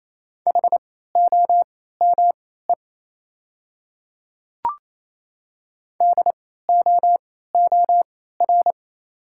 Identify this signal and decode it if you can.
Morse code, heard as a keyed tone, about 27 wpm, spaced as for 16 wpm, 700 Hz. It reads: HOME DOOR
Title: 3x Character Spacing - Top 300 Words - 28wpm